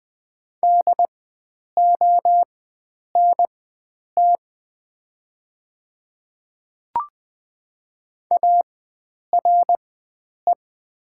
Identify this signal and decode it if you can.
Morse code, heard as a keyed tone, about 20 wpm, spaced as for 9 wpm, 700 Hz. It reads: DONT ARE
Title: 4x Character Spacing - Top 300 Words - 20wpm